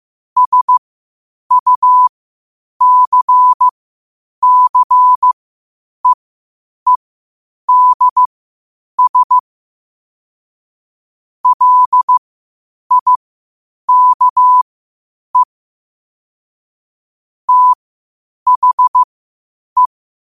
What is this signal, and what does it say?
Morse code, heard as a keyed tone, about 15 wpm, spaced as for 8 wpm, 1000 Hz. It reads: SUCCEEDS LIKE THE
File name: qotdfiles / 8wpm